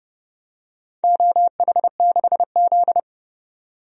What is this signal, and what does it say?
Morse code, heard as a keyed tone, about 29 wpm, 700 Hz. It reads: OH6Z
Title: DXCC-100 Call Signs - 4 Characters - 30wpm